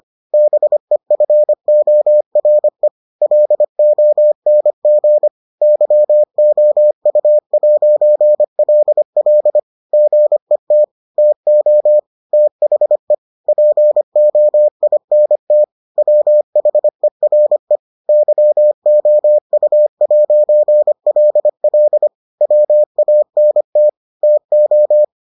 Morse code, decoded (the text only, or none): BEFORE LONG YOU'LL GET TO THE POINT WHERE YOU'LL WANT TO